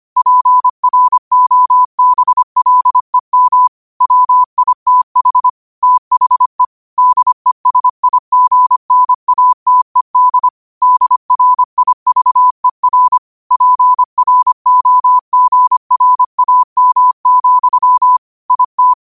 Morse code, heard as a keyed tone, about 25 wpm, 1000 Hz. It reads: PROBLEM WITH THE DESIGNATED DRIVER PROGRAM, IT
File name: qotdfiles / 25wpm